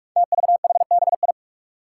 Morse code, heard as a keyed tone, about 44 wpm, 700 Hz. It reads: TVHBI
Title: Five Challenging Letter-Number Combinations 45wpm